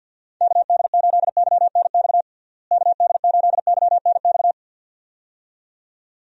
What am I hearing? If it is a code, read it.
Morse code, heard as a keyed tone, about 47 wpm, 700 Hz. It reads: KD8YNX KD8YNX
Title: USA Call Signs - 6 Characters - Repeated 2x - 50wpm